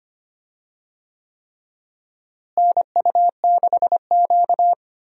Morse code, heard as a keyed tone, about 25 wpm, 700 Hz. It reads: NU6Q
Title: USA Call Signs - 4 Characters - Repeated 3x - 25wpm